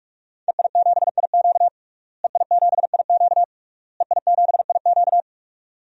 Morse code, heard as a keyed tone, about 46 wpm, 700 Hz. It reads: EI7IQ EI7IQ EI7IQ
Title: DXCC-100 Call Signs - 5 Characters - Repeated 3x - 45wpm